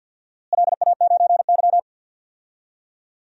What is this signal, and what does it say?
Morse code, heard as a keyed tone, about 52 wpm, 700 Hz. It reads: LA9Y
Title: DXCC-100 Call Signs - 4 Characters - Repeated 2x - 50wpm